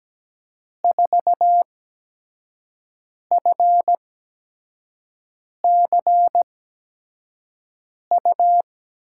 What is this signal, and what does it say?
Morse code, heard as a keyed tone, about 17 wpm, spaced as for 5 wpm, 700 Hz. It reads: 4FCU
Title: USA Call Signs - 6 Characters - 8x Character Spacing - 17wpm